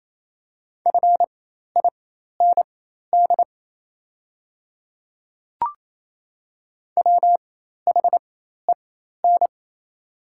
Morse code, heard as a keyed tone, about 28 wpm, spaced as for 13 wpm, 700 Hz. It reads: FIND WHEN
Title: 4x Character Spacing - Top 100 Words - 28wpm